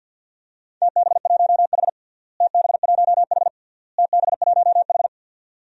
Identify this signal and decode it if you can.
Morse code, heard as a keyed tone, about 51 wpm, 700 Hz. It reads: TB1H TB1H TB1H
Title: DXCC-100 Call Signs - 4 Characters - Repeated 3x - 50wpm